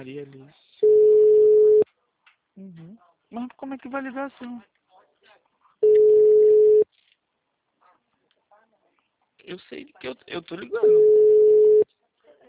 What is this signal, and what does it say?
Ringback tone (Europe, 425 Hz), with speech.
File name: out- -2009- - - .0